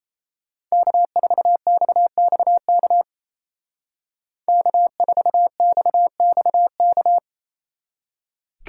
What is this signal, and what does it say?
Morse code, heard as a keyed tone, about 30 wpm, 700 Hz. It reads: K4XXK K4XXK
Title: Speed Racing - USA 1x3 Call Signs 28wpm